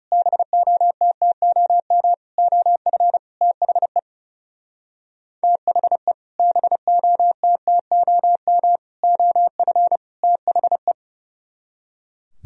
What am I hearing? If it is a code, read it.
Morse code, heard as a keyed tone, about 38 wpm, 700 Hz. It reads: BOTTOM OF THE THE BOTTOM OF THE